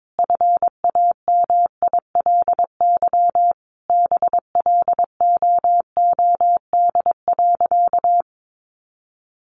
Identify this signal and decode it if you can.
Morse code, heard as a keyed tone, about 22 wpm, 700 Hz. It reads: FAMILY BLOOD.
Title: Sentences from Top 1200 Words 22wpm